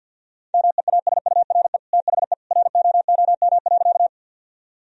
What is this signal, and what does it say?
Morse code, heard as a keyed tone, about 50 wpm, 700 Hz. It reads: MEASURE THE ROOM.
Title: Sentences from Top 1000 Words - Review Part 2 - 50wpm